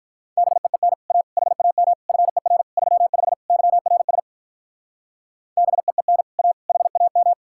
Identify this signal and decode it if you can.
Morse code, heard as a keyed tone, about 44 wpm, 700 Hz. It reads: BEEN A HAM FER 35 YRS BEEN A HAM